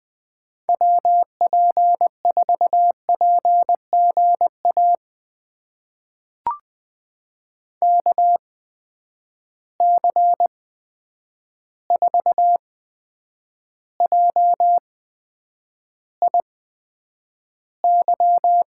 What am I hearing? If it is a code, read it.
Morse code, heard as a keyed tone, about 20 wpm, spaced as for 9 wpm, 700 Hz. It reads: WP4PGA KC4JIY
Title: USA Call Signs - 6 Characters - 8x Character Spacing - 20wpm